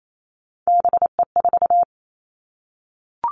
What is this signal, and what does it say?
Morse code, heard as a keyed tone, about 28 wpm, 700 Hz. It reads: BE4
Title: Three Challenging Letter-Number Combinations 28wpm